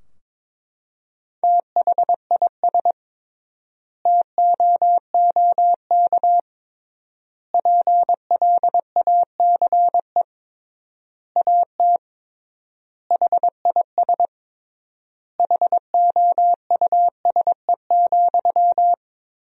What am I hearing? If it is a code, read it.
Morse code, heard as a keyed tone, about 21 wpm, 700 Hz. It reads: THIS TOOK PLACE AT HIS HOUSE,
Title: 3x Word-Spacing - Sentences from Top 300 Words - 22wpm